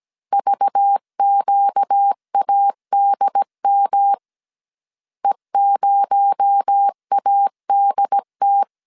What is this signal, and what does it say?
Morse code, heard as a keyed tone, about 17 wpm, 800 Hz. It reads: VQADM E0ADT
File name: 16wpm